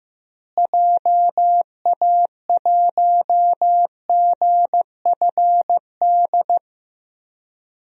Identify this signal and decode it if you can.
Morse code, heard as a keyed tone, about 15 wpm, 700 Hz. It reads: JA1GFD
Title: DXCC-100 Call Signs - 6 Characters - Repeated 3x - 15wpm